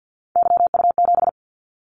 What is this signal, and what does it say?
Morse code, heard as a keyed tone, about 50 wpm, 700 Hz. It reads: YU7
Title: Callsign Prefixes for Top 25 DXCC - 50wpm